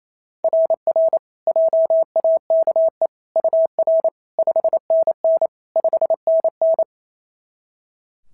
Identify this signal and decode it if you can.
Morse code, heard as a keyed tone, about 29 wpm, 660 Hz. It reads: RR JAKE UR 5NN 5NN
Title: SOTA QSO Practice 28wpm